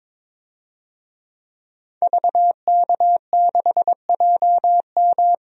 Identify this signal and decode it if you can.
Morse code, heard as a keyed tone, about 22 wpm, 700 Hz. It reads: VK6JM
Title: DXCC-100 Call Signs - 5 Characters - Repeated 2x - 22wpm